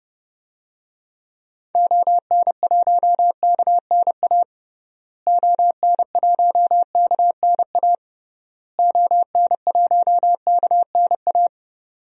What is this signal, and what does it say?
Morse code, heard as a keyed tone, about 30 wpm, 700 Hz. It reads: ON1KNA ON1KNA ON1KNA
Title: DXCC-100 Call Signs - 6 Characters - Repeated 3x - 30wpm